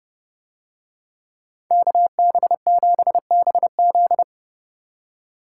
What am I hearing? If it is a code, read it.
Morse code, heard as a keyed tone, about 30 wpm, 700 Hz. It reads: KB7BZ